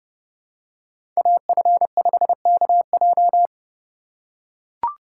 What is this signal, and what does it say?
Morse code, heard as a keyed tone, about 31 wpm, 700 Hz. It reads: AF5KJ
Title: USA Call Signs - 5 Characters - 2x Character Spacing - 30wpm